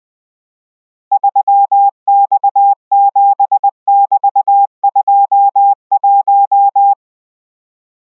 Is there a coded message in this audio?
Yes: Morse code from 1.1 to 6.9 s, heard as a keyed tone, about 20 wpm, 800 Hz. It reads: 3X7<BT>21